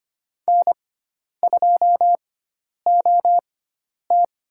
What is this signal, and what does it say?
Morse code, heard as a keyed tone, about 25 wpm, spaced as for 10 wpm, 700 Hz. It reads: N2OT
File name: Lesson 16g - Letter c - Call Signs 25wpm